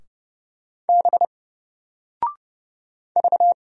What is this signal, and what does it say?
Morse code, heard as a keyed tone, about 30 wpm, 700 Hz. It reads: B V
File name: Single-Letter-Number-30wpm